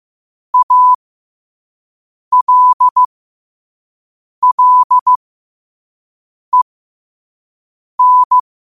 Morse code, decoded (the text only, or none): ALLEN